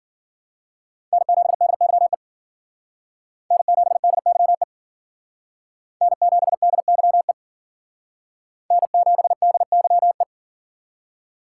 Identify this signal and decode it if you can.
Morse code, heard as a keyed tone, about 53 wpm, 700 Hz. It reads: N7DYE N7DYE N7DYE N7DYE